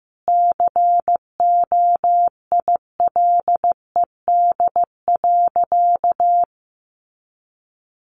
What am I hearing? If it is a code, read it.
Morse code, heard as a keyed tone, about 15 wpm, 700 Hz. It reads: COILED.
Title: A Sound Of Thunder 15wpm